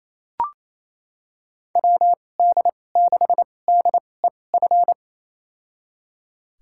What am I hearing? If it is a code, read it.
Morse code, heard as a keyed tone, about 28 wpm, spaced as for 20 wpm, 700 Hz. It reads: WD6DEF
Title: USA Call Signs - 6 Characters - 2x Character Spacing - 28wpm